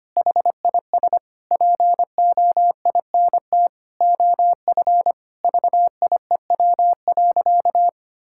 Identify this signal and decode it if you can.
Morse code, heard as a keyed tone, about 25 wpm, 700 Hz. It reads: HIS POINT OF VIEW.
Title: Sentences from Top 1000 Words - Review Part 6 - 25wpm